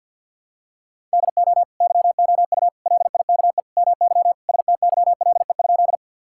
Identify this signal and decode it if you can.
Morse code, heard as a keyed tone, about 51 wpm, 700 Hz. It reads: DO YOU LIKE MY STYLE?
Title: Sentences from Top 1000 Words - Review Part 5 - 50wpm